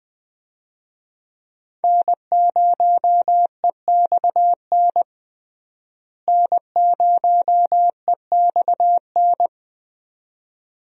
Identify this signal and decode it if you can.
Morse code, heard as a keyed tone, about 20 wpm, 700 Hz. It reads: N0EXN N0EXN
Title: USA Call Signs - 5 Characters - Repeated 2x - 20wpm